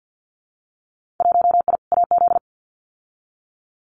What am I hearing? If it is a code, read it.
Morse code, heard as a keyed tone, about 50 wpm, 700 Hz. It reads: 1I AZ
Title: Winter Field Day Exchanges 50wpm